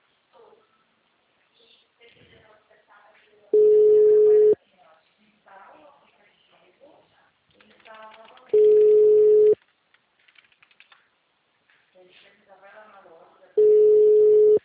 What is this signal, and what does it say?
Ringback tone (Europe, 425 Hz), with speech.
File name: out- -2007- - - .6948